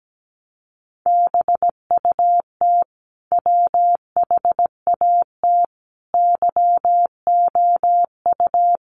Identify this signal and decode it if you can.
Morse code, heard as a keyed tone, about 17 wpm, 700 Hz. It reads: BUT WHAT YOU
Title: Sentences from Top 1300 Words 17wpm